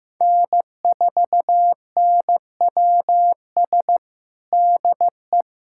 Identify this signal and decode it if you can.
Morse code, heard as a keyed tone, about 15 wpm, 700 Hz. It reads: N4NWS DE
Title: USA Callsign DE Callsign - 5 Characters - 15wpm